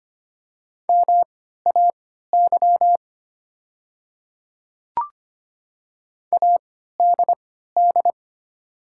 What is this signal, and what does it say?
Morse code, heard as a keyed tone, about 25 wpm, spaced as for 14 wpm, 700 Hz. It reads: MAY ADD
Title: 3x Character Spacing - Top 300 Words - 25wpm